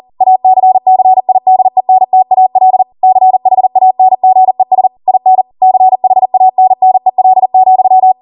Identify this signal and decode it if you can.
Morse code, heard as a keyed tone, about 40 wpm, 750 Hz. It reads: ACCIDENTAL CHANGES IN CHANNEL,
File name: Morse / 40WPM